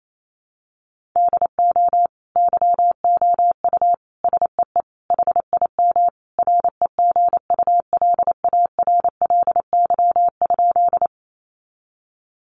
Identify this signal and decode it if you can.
Morse code, heard as a keyed tone, about 28 wpm, 700 Hz. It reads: DO YOU SEE HIM REGULARLY?
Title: Sentences from Top 1400 Words 28wpm